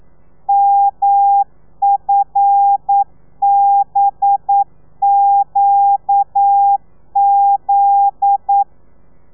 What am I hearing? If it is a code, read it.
Morse code, heard as a keyed tone, about 9 wpm, 800 Hz. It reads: MFBQZ
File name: telegrafia / i2tfj / Lez9 40